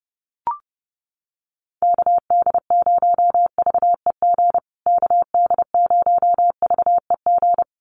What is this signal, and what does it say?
Morse code, heard as a keyed tone, about 30 wpm, 700 Hz. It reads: KD0VEG KD0VEG
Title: USA Call Signs Repeated 2x 30wpm